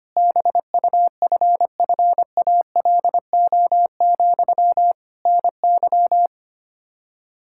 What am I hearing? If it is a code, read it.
Morse code, heard as a keyed tone, about 25 wpm, 700 Hz. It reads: BUFFALO, NY
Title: Practice QSO - Exchange 1 - Beginning - 25wpm